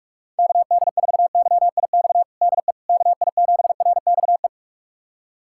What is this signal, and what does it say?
Morse code, heard as a keyed tone, about 46 wpm, 700 Hz. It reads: KD4YIX DE KI7RXE
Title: USA Callsign DE Callsign - 6 Characters - 45wpm